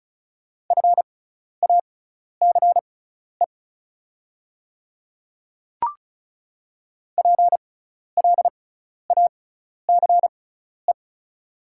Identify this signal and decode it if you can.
Morse code, heard as a keyed tone, about 36 wpm, spaced as for 12 wpm, 690 Hz. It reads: FACE PLACE
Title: 6x Character Spacing - Top 300 Words - 35wpm